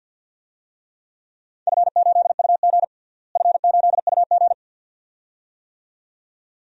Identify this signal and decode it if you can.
Morse code, heard as a keyed tone, about 51 wpm, 700 Hz. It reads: F8UG F8UG
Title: DXCC-100 Call Signs - 4 Characters - Repeated 2x - 50wpm